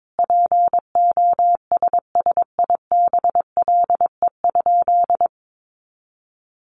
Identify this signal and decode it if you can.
Morse code, heard as a keyed tone, about 22 wpm, 700 Hz. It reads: POSSIBLE?
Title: Sentences from Top 500 Words 22wpm